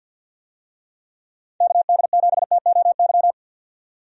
Morse code, heard as a keyed tone, about 52 wpm, 700 Hz. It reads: KD7TOY
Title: USA Call Signs - 6 Characters - 50wpm